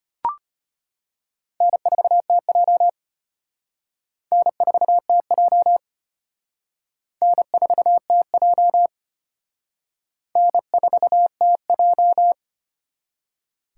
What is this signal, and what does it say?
Morse code, heard as a keyed tone, about 33 wpm, 700 Hz. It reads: N4TJ N4TJ N4TJ N4TJ